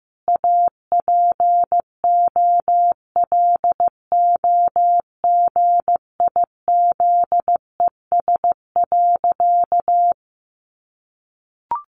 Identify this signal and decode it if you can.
Morse code, heard as a keyed tone, about 15 wpm, 700 Hz. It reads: APOLOGIZES.
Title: Additional Easy Phrases 15wpm